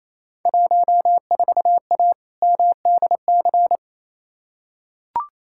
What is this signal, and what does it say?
Morse code, heard as a keyed tone, about 28 wpm, 700 Hz. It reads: 14A MDC
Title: Field Day Exchanges Encore - 28wpm